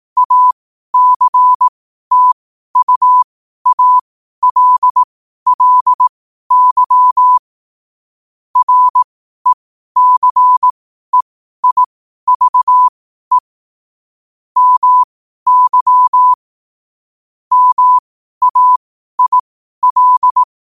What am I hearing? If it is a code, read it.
Morse code, heard as a keyed tone, about 18 wpm, spaced as for 12 wpm, 1000 Hz. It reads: ACTUALLY RECEIVE MY MAIL
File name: qotdfiles / 12wpm